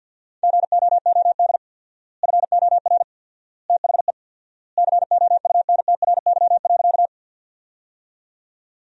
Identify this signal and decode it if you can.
Morse code, heard as a keyed tone, about 50 wpm, 700 Hz. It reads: GOOD FOR THE COUNTRY.
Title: 4x Word-Spacing - Sentences from Top 300 Words - 50wpm